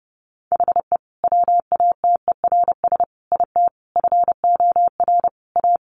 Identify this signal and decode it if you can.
Morse code, heard as a keyed tone, about 30 wpm, 700 Hz. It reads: HE WATERS IT FOR A
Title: Easy Phrases 30wpm